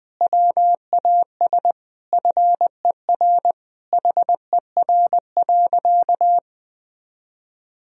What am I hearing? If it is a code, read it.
Morse code, heard as a keyed tone, about 21 wpm, 700 Hz. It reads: WAS FER HER.
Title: Sentences from Top 400 Words and Abbreviations 20wpm